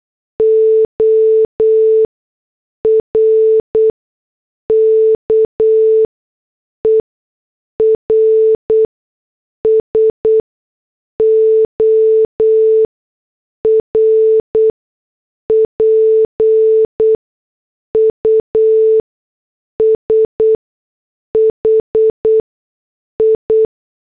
Morse code, heard as a keyed tone, about 8 wpm, spaced as for 6 wpm, 440 Hz. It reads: ORKERS OR PUSHI